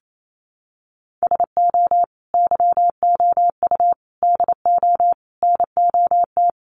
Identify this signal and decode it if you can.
Morse code, heard as a keyed tone, about 28 wpm, 700 Hz. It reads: SO YOU DO NOT
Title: Sets of 5 Words Encore 28wpm